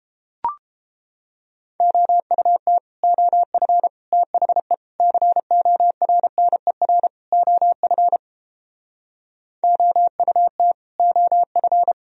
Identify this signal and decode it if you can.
Morse code, heard as a keyed tone, about 31 wpm, 700 Hz. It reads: OUT OF THE CORNER OF OUT OF